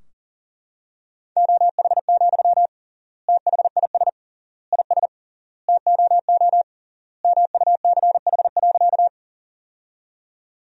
Morse code, heard as a keyed tone, about 40 wpm, 700 Hz. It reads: OH, THIS IS TOO MUCH.
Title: 3x Word-Spacing - Sentences from Top 500 Words - 40wpm